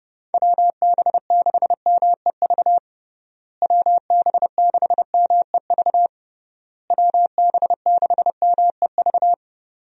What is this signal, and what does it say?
Morse code, heard as a keyed tone, about 32 wpm, 700 Hz. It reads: WB6MEV WB6MEV WB6MEV
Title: USA Call Signs - 6 Characters - Repeated 3x - 30wpm